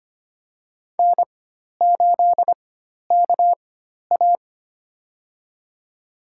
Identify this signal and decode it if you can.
Morse code, heard as a keyed tone, about 25 wpm, spaced as for 12 wpm, 700 Hz. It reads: N8KA
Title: USA Call Signs - 4 Characters - 4x Character Spacing - 25wpm